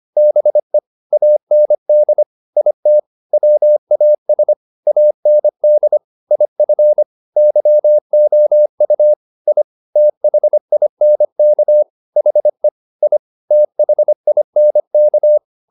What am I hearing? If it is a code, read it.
Morse code, heard as a keyed tone, about 25 wpm, 600 Hz. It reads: BE AND IT WAS AND IF YOU I THINK HE I THINK